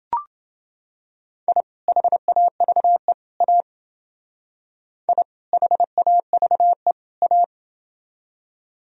Morse code, heard as a keyed tone, about 29 wpm, 700 Hz. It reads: I HAVE A I HAVE A